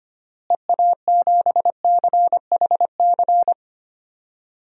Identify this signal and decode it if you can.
Morse code, heard as a keyed tone, about 25 wpm, 700 Hz. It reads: EA7CHC
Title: DXCC-100 Call Signs - 6 Characters - Repeated 3x - 25wpm